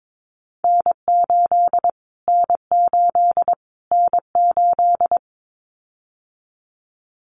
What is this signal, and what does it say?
Morse code, heard as a keyed tone, about 22 wpm, 700 Hz. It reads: N8 N8 N8
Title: Callsign Prefixes for Top 25 DXCC - Repeated 3x - 22wpm